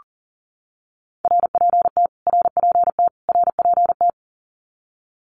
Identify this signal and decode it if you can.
Morse code, heard as a keyed tone, about 40 wpm, 700 Hz. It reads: RPT RPT RPT
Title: Abbreviations Repeated 3x - 40wpm